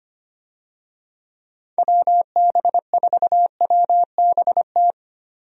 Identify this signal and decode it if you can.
Morse code, heard as a keyed tone, about 25 wpm, 700 Hz. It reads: WB4WBT
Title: USA Call Signs - 6 Characters - Repeated 3x - 25wpm